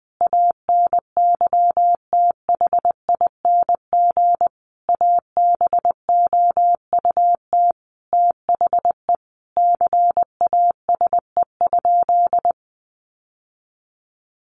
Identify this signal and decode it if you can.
Morse code, heard as a keyed tone, about 20 wpm, 700 Hz. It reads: ANYTHING ABOUT THE CASE?
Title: Sentences from Top 1000 Words 20wpm